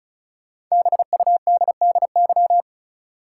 Morse code, heard as a keyed tone, about 36 wpm, 700 Hz. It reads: BUDDY
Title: Five Letter Words - 35wpm